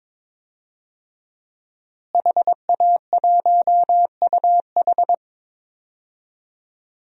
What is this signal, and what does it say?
Morse code, heard as a keyed tone, about 22 wpm, 700 Hz. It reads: HA1UH
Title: DXCC-100 Call Signs - 5 Characters - Repeated 3x - 22wpm